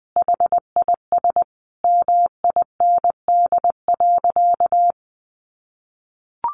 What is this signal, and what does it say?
Morse code, heard as a keyed tone, about 20 wpm, 700 Hz. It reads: HIS MIND.
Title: Sentences from Top 1400 Words 20wpm